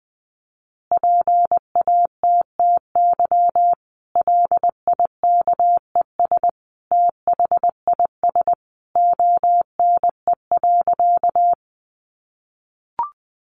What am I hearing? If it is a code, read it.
Morse code, heard as a keyed tone, about 19 wpm, 700 Hz. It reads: PATTY LIKES THIS ONE.
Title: Additional Easy Phrases 20wpm